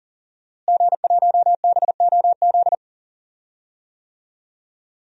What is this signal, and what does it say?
Morse code, heard as a keyed tone, about 40 wpm, 700 Hz. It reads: G1BOZ
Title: DXCC-100 Call Signs - 5 Characters - Repeated 3x - 40wpm